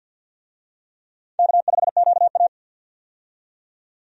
Morse code, heard as a keyed tone, about 51 wpm, 700 Hz. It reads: K5QA
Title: USA Call Signs - 4 Characters - 50wpm